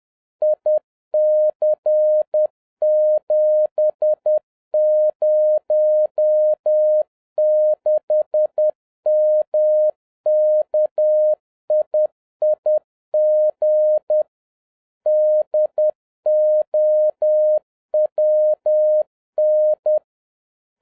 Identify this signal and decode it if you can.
Morse code, heard as a keyed tone, about 10 wpm, 610 Hz. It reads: IC706MKIIG DOWN
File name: 2011-07-mp3 / n41 EXCITING QRP CW OPERATION 10wpm